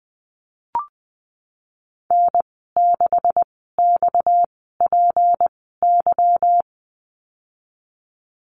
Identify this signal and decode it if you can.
Morse code, heard as a keyed tone, about 20 wpm, spaced as for 15 wpm, 700 Hz. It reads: N6XPY
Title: 238 Lesson 38g - Letter q - Call Signs 20wpm